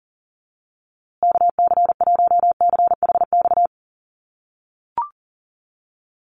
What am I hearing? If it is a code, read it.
Morse code, heard as a keyed tone, about 40 wpm, 700 Hz. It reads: KC1CHX
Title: USA Call Signs Repeated 3x 40wpm